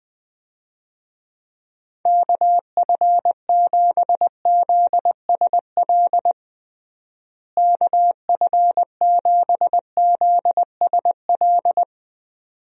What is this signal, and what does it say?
Morse code, heard as a keyed tone, about 20 wpm, 700 Hz. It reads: KF7ZSL KF7ZSL
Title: USA Call Signs - 6 Characters - Repeated 2x - 20wpm